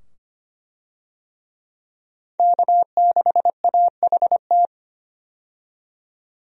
Morse code, heard as a keyed tone, about 25 wpm, 700 Hz. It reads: K6AHT
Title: USA Call Signs - 5 Characters - 25wpm